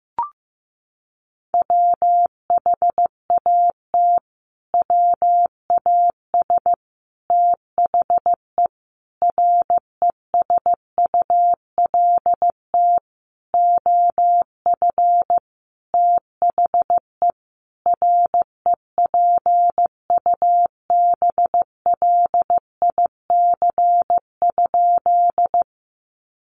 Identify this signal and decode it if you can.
Morse code, heard as a keyed tone, about 15 wpm, 700 Hz. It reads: WHAT WAS THE RESULT OF THE REPUBLIC?
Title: Sentences from Top 1500 Words 15wpm